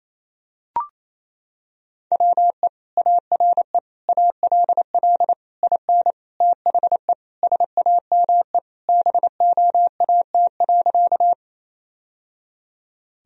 Morse code, heard as a keyed tone, about 28 wpm, 700 Hz. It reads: WE ARE ALL IN THE SAME BOAT.
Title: Sentences from Top 1000 Words - Review Part 3 - 28wpm